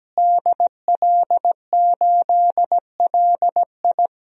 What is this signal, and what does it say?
Morse code, heard as a keyed tone, about 17 wpm, 700 Hz. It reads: DL8LI
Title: DXCC-100 Call Signs - 5 Characters - Repeated 3x - 17wpm